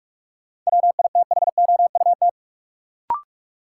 Morse code, heard as a keyed tone, about 46 wpm, 700 Hz. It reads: WITHOUT
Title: Sentences from Top 600 Words - New Word Drill - 45wpm